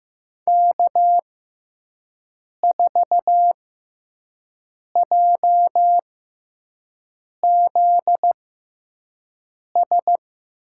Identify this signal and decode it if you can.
Morse code, heard as a keyed tone, about 15 wpm, spaced as for 5 wpm, 700 Hz. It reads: K4JZS
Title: USA Call Signs - 5 Characters - 6x Character Spacing - 15wpm